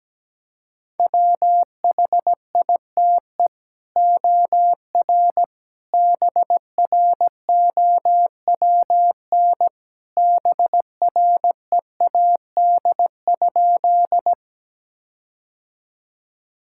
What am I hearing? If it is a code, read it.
Morse code, heard as a keyed tone, about 17 wpm, 700 Hz. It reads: WHITE OR BROWN BREAD?
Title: Sentences from Top 1600 Words 17wpm